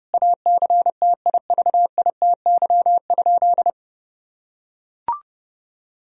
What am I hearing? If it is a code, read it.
Morse code, heard as a keyed tone, about 30 wpm, 700 Hz. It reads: ACTIVITY?
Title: Sentences from Top 1000 Words - Review Part 4 - 30wpm